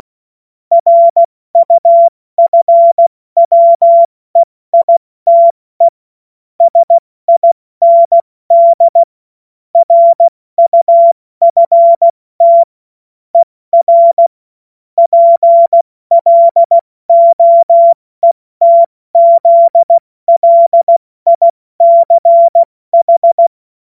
Morse code, heard as a keyed tone, about 16 wpm, spaced as for 14 wpm, 680 Hz. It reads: RUFWEITE SIND RUFT ER PLOETZLICH
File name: Morsekurs / Witze / Witz 03. Mai 2022 70 F80